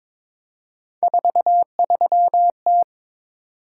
Morse code, heard as a keyed tone, about 22 wpm, 700 Hz. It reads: 43T
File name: practice / 22wpm / Three Letter-Number Combinations - 22wpm